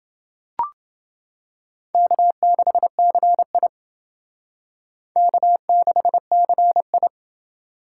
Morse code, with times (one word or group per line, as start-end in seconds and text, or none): 1.9-3.7 K6CI
5.2-7.1 K6CI